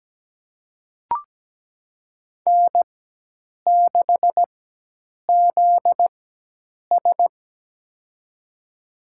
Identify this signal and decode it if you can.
Morse code, heard as a keyed tone, about 17 wpm, spaced as for 8 wpm, 700 Hz. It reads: N6ZS
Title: USA Call Signs - 4 Characters - 4x Character Spacing - 17wpm